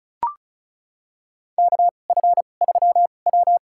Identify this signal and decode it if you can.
Morse code, heard as a keyed tone, about 36 wpm, spaced as for 25 wpm, 700 Hz. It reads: KF3W
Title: USA Call Signs - 4 Characters - 2x Character Spacing - 35wpm